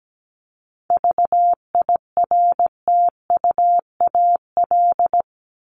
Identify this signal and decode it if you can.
Morse code, heard as a keyed tone, about 17 wpm, 700 Hz. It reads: VIRTUAL
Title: LY Suffix - Top 100 Words with Base Words 17wpm